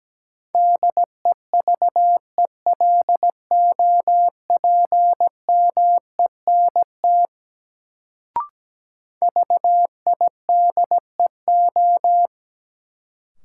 Morse code, heard as a keyed tone, about 17 wpm, 700 Hz. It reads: DEVELOPMENT VIDEO
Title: Sentences from Top 1000 Words - Review Part 4 - New Word Drill - 17wpm